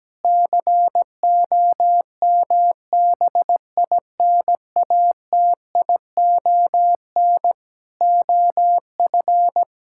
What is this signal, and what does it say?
Morse code, heard as a keyed tone, about 17 wpm, 700 Hz. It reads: COMBINATION OF
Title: Sentences from Top 1000 Words - Review Part 6 - 17wpm